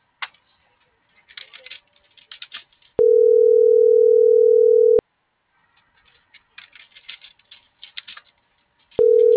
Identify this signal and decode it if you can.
Ringback tone (North America), 440 + 480 Hz.